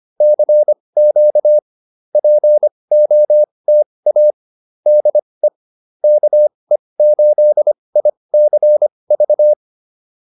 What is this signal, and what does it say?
Morse code, heard as a keyed tone, about 25 wpm, spaced as for 20 wpm, 600 Hz. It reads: CQ POTA DE KE8ICV